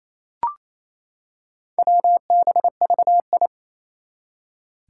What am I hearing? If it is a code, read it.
Morse code, heard as a keyed tone, about 27 wpm, 700 Hz. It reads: WBVI
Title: Four Challenging Letter-Number Combinations 28wpm